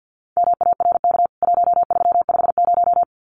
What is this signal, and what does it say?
Morse code, heard as a keyed tone, about 50 wpm, 700 Hz. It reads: MARK 1350
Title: CWT Exchange 50wpm